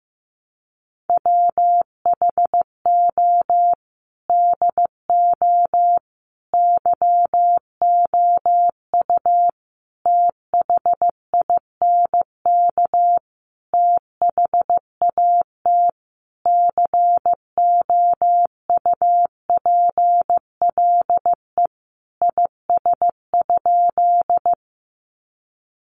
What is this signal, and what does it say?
Morse code, heard as a keyed tone, about 15 wpm, 700 Hz. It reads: WHO DO YOU THINK THAT COUPLE IS?
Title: Sentences from Top 1100 Words 15wpm